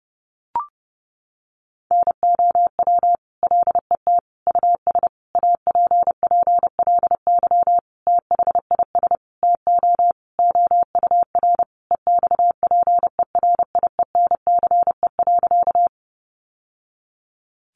Morse code, heard as a keyed tone, about 30 wpm, 700 Hz. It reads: NOW LET US APPLY THIS TO OUR EXPERIENCE.
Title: Sentences from Top 1200 Words 30wpm